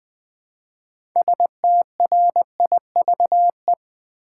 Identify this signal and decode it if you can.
Morse code, heard as a keyed tone, about 20 wpm, 700 Hz. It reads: STRIVE
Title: 6 Letter Words - First 3 Letters Given - 20wpm